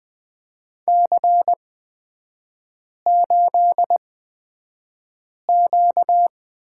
Morse code, heard as a keyed tone, about 20 wpm, spaced as for 5 wpm, 690 Hz. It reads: C8Q
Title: 236 Lesson 38e - Letter q - Three Characters 20wpm